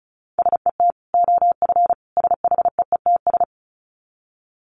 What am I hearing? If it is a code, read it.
Morse code, heard as a keyed tone, about 34 wpm, 700 Hz. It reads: SET OF SHEETS
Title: Dit Challenge 35wpm